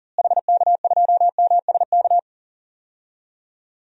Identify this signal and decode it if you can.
Morse code, heard as a keyed tone, about 40 wpm, 700 Hz. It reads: HK2MSK
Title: DXCC-100 Call Signs - 6 Characters - Repeated 3x - 40wpm